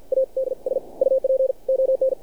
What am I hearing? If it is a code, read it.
Morse code, heard as a keyed tone, about 53 wpm, spaced as for 46 wpm, 530 Hz. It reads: ADS UP ON
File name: archived / CRSnet